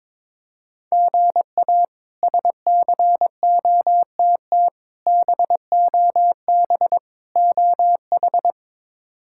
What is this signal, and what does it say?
Morse code, heard as a keyed tone, about 22 wpm, 700 Hz. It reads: GA SCOTT BOB OH
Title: K1USN SST QSO Practice - 22wpm